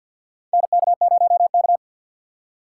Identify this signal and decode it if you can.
Morse code, heard as a keyed tone, about 50 wpm, 700 Hz. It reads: NK0K
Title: USA Call Signs - 4 Characters - 2x Character Spacing - 50wpm